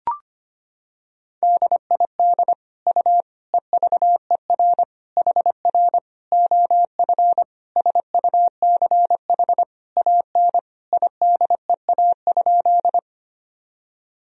Morse code, heard as a keyed tone, about 25 wpm, 700 Hz. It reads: DID U EVER HR OF SUCH AN IDEA?
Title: Sentences from Top 400 Words and Abbreviations 25wpm